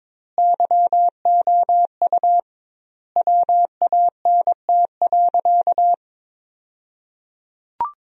Morse code, heard as a keyed tone, about 23 wpm, 700 Hz. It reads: YOU WANT.
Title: 2x Word-Spacing - Sentences from Top 200 Words - 22wpm